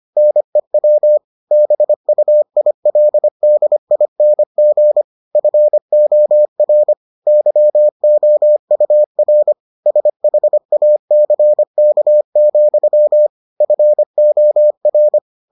Morse code, heard as a keyed tone, about 25 wpm, 600 Hz. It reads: NEW BUILDING FOR YOUR SHACK, FOR